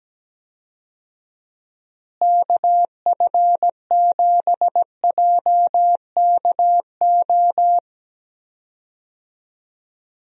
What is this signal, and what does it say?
Morse code, heard as a keyed tone, about 17 wpm, 700 Hz. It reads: KF7JKO
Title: USA Call Signs - 6 Characters - Repeated 2x - 17wpm